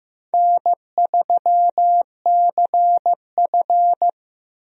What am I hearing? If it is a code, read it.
Morse code, heard as a keyed tone, about 15 wpm, 700 Hz. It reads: N3CF
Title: USA Call Signs - 4 Characters - 15wpm